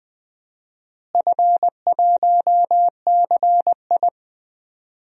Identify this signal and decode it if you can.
Morse code, heard as a keyed tone, about 20 wpm, 700 Hz. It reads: F1CI
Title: DXCC-100 Call Signs - 4 Characters - Repeated 3x - 20wpm